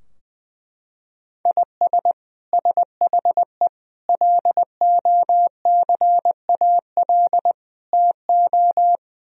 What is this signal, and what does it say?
Morse code, heard as a keyed tone, about 21 wpm, 700 Hz. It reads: IS SHE LOCAL TO
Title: Sentences from Top 1000 Words - Review Part 3 - 20wpm